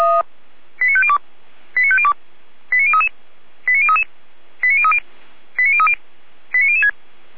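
Five-tone selective call (ZVEI), heard as quick runs of five tones, tones of about 70 ms. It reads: 89502 89502 89033 89033 89030 89030 89006
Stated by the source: ZVEI five-tone call received by SDR, archived by Feuerwehr Orlamünde (fire brigade).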